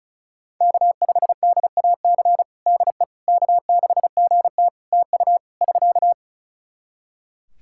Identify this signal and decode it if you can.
Morse code, heard as a keyed tone, about 36 wpm, 700 Hz. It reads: K5DAC DE K6GT TU <SK>
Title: Practice QSO - Exchange 4 - Ending - 35wpm